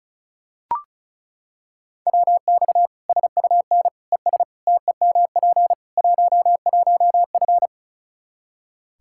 Morse code, heard as a keyed tone, about 36 wpm, 700 Hz. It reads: WX SUN ES TEMP 11F
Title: Practice QSO - Exchange 2 - Info - 35wpm